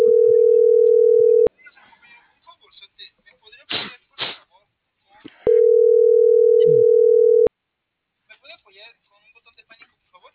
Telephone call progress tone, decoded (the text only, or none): ringback tone